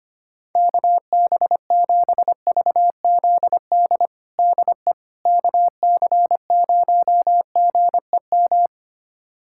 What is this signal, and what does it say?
Morse code, heard as a keyed tone, about 25 wpm, 700 Hz. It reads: KB7VZD DE KC0GEM
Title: USA Callsign DE Callsign 25wpm